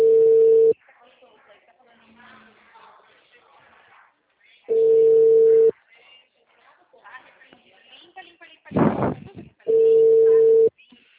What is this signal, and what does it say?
Ringback tone (China, 450 Hz), with speech.
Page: Cordeiropolis